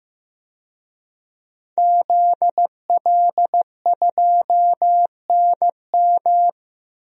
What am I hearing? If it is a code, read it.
Morse code, heard as a keyed tone, about 15 wpm, 700 Hz. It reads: ZL2NM
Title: DXCC-100 Call Signs - 5 Characters - 15wpm